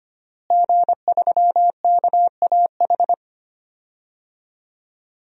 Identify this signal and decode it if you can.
Morse code, heard as a keyed tone, about 25 wpm, 700 Hz. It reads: G3KAH
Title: DXCC-100 Call Signs - 5 Characters - Repeated 2x - 25wpm